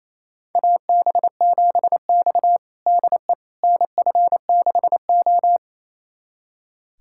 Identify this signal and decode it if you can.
Morse code, heard as a keyed tone, about 28 wpm, 700 Hz. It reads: AB7X DE NF6O
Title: USA Callsign DE Callsign - 4 Characters - 28wpm